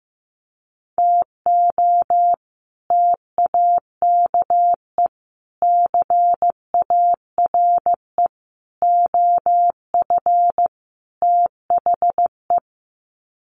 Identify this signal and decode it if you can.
Morse code, heard as a keyed tone, about 14 wpm, 700 Hz. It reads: TO TAKE CARE OF THE
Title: Sets of 5 Words Encore 15wpm